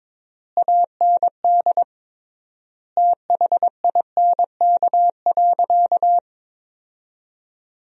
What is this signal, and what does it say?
Morse code, heard as a keyed tone, about 22 wpm, 700 Hz. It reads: AND THINK.
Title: 3x Word-Spacing - Sentences from Top 500 Words - 22wpm